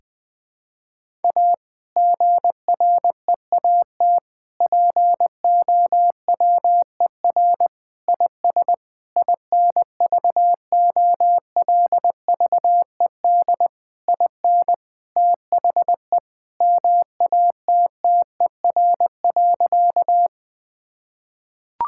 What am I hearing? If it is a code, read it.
Morse code, heard as a keyed tone, about 21 wpm, 700 Hz. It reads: A GREAT POWER IS INVOLVED IN THE MATTER.
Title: Sentences from Top 1000 Words - Review Part 5 - 20wpm